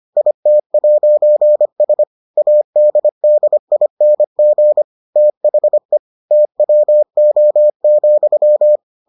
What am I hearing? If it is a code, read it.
Morse code, heard as a keyed tone, about 25 wpm, 600 Hz. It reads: IT'S ADDING THE TWO,